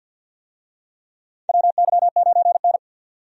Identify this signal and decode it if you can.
Morse code, heard as a keyed tone, about 48 wpm, 700 Hz. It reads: WY9N
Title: USA Call Signs - 4 Characters - 50wpm